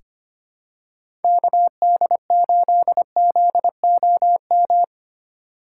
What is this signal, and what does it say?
Morse code, heard as a keyed tone, about 25 wpm, 700 Hz. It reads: KD8ZOM
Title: USA Call Signs - 6 Characters - 4x Character Spacing - 25wpm